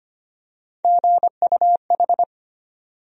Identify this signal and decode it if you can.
Morse code, heard as a keyed tone, about 25 wpm, 700 Hz. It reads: GUH
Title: USA Call Sign Postfixes - 3 Letters 25wpm